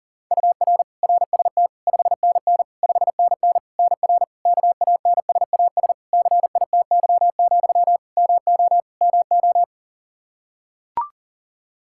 Morse code, heard as a keyed tone, about 40 wpm, 700 Hz. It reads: UR RST 5NN 5NN NR KANSAS CITY, MO MO
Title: Practice QSO - Exchange 1 - Beginning - 40wpm